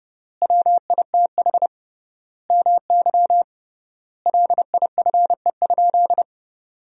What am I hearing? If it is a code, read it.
Morse code, heard as a keyed tone, about 32 wpm, 700 Hz. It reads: WITH MY LIFE?
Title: 3x Word-Spacing - Sentences from Top 300 Words - 30wpm